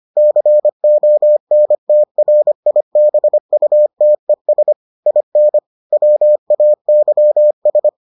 Morse code, heard as a keyed tone, about 25 wpm, 600 Hz. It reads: CONTRIBUTES IN WAYS